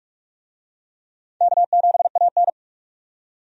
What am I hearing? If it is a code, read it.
Morse code, heard as a keyed tone, about 42 wpm, 700 Hz. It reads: K7AN
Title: USA Call Signs - 4 Characters - 45wpm